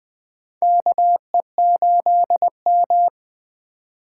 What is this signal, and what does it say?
Morse code, heard as a keyed tone, about 20 wpm, 700 Hz. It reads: KE8M
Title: K1USN SST QSO Practice - 20wpm